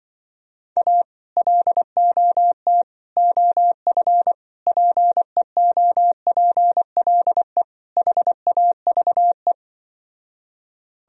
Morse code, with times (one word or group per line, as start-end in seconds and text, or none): 0.8-1.0 A
1.4-2.8 LOT
3.2-4.3 OF
4.7-7.6 PEOPLE
8.0-9.5 HAVE